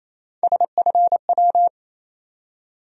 Morse code, heard as a keyed tone, about 27 wpm, 700 Hz. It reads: SFW
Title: USA Call Sign Postfixes - 3 Letters 28wpm